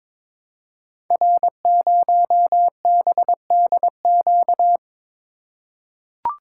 Morse code, heard as a keyed tone, about 22 wpm, 700 Hz. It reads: R0BDQ
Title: Code Groups - 5 Letter:Numbers - 22wpm